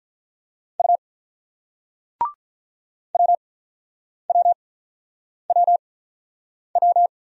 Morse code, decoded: U WWWW

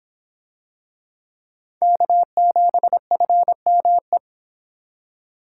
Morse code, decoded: K7FME

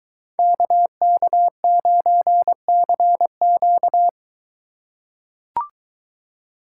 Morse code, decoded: KK9CQ

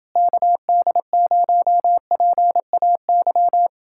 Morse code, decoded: KD0PAY

